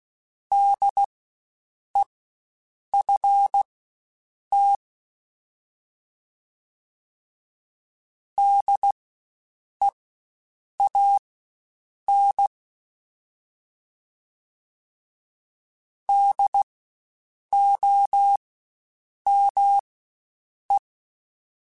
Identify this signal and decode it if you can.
Morse code, heard as a keyed tone, about 16 wpm, spaced as for 7 wpm, 800 Hz. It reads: DEFT DEAN DOME
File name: Spaced Alphas / Spaced Alphas mp3s / Ds sp @15